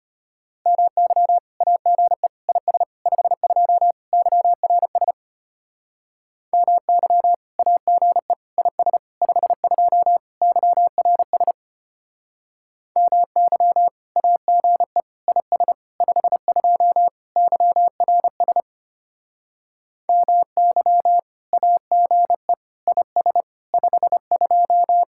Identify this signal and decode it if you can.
Morse code, heard as a keyed tone, about 33 wpm, 700 Hz. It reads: MY AGE IS 52 YRS MY AGE IS 52 YRS MY AGE IS 52 YRS MY AGE IS 52